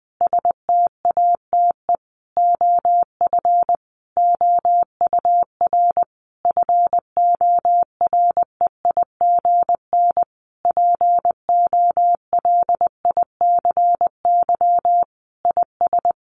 Morse code, heard as a keyed tone, about 19 wpm, 700 Hz. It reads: STATE OF OUR FOREIGN POLICY IS
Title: Sentences from Top 1400 Words 20wpm